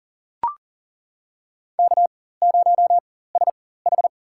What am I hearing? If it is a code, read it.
Morse code, heard as a keyed tone, about 42 wpm, spaced as for 19 wpm, 700 Hz. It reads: K0SH
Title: USA Call Signs - 4 Characters - 4x Character Spacing - 40wpm